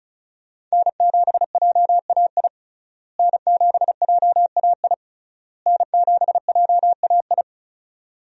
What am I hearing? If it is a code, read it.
Morse code, heard as a keyed tone, about 36 wpm, 700 Hz. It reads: N7JAI N7JAI N7JAI
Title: USA Call Signs - 5 Characters - Repeated 3x - 35wpm